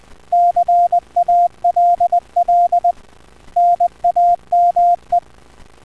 Morse code, heard as a keyed tone, about 20 wpm, 700 Hz. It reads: CALL NAME